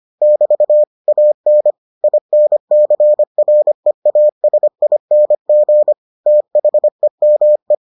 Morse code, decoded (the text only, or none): <BT> AN INCREASING THEME